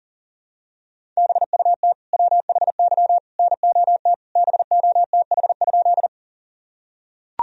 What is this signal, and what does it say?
Morse code, heard as a keyed tone, about 40 wpm, 700 Hz. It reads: BUT WHY NOT BOTH?
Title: Sentences from Top 1000 Words - Review Part 1 - 40wpm